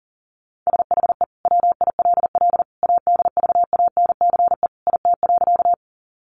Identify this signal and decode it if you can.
Morse code, heard as a keyed tone, about 40 wpm, 700 Hz. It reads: SHE WILL ADVANCE IT.
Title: Sentences from Top 1400 Words 40wpm